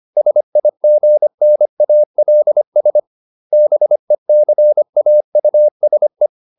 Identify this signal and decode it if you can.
Morse code, heard as a keyed tone, about 25 wpm, 600 Hz. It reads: SIGNALS BECAUSE